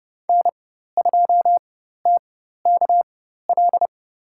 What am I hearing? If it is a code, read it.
Morse code, heard as a keyed tone, about 30 wpm, spaced as for 14 wpm, 700 Hz. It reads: N2TKL
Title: USA Call Signs - 5 Characters - 4x Character Spacing - 30wpm